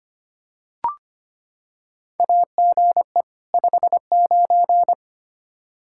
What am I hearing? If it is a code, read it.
Morse code, heard as a keyed tone, about 25 wpm, 700 Hz. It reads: AGE 59
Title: Practice QSO - Exchange 3 - More Info - 25wpm